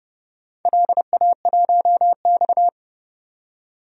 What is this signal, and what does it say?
Morse code, heard as a keyed tone, about 30 wpm, 700 Hz. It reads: LA1X
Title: DXCC-100 Call Signs - 4 Characters - 30wpm